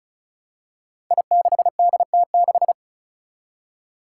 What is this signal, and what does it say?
Morse code, heard as a keyed tone, about 36 wpm, 700 Hz. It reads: I6DT6
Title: Five Challenging Letter-Number Combinations 35wpm